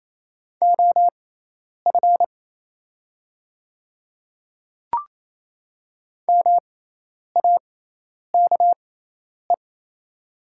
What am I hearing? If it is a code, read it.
Morse code, heard as a keyed tone, about 28 wpm, spaced as for 10 wpm, 690 Hz. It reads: OF MAKE
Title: 6x Character Spacing - Top 100 Words - 28wpm